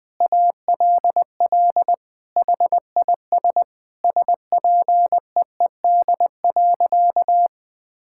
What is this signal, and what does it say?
Morse code, heard as a keyed tone, about 20 wpm, 700 Hz. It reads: ALL HIS SPEED.
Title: Sentences from Top 1000 Words - Review Part 5 - 20wpm